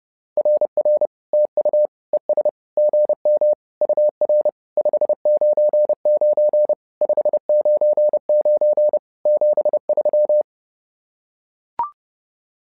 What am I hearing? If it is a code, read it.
Morse code, heard as a keyed tone, about 30 wpm, 600 Hz. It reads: RR TU ES GM UR 599 599 73
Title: SOTA QSO Practice 30wpm